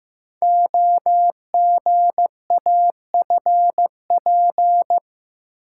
Morse code, heard as a keyed tone, about 15 wpm, 700 Hz. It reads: OGAFP
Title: Code Groups - 5 Letters - 15wpm